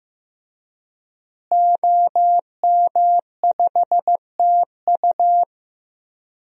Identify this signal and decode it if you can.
Morse code, heard as a keyed tone, about 15 wpm, 700 Hz. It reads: OM5TU
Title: DXCC-100 Call Signs - 5 Characters - Repeated 3x - 15wpm